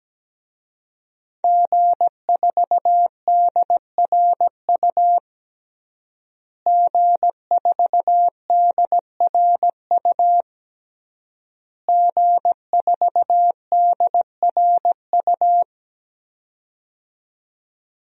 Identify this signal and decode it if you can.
Morse code, heard as a keyed tone, about 17 wpm, 700 Hz. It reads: G4DRU G4DRU G4DRU
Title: DXCC-100 Call Signs - 5 Characters - Repeated 3x - 17wpm